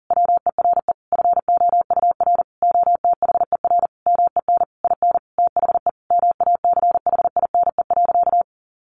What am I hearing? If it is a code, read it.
Morse code, heard as a keyed tone, about 40 wpm, 700 Hz. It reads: WERE FOUR OTHER MEN IN THE MACHINE.
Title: A Sound Of Thunder 40wpm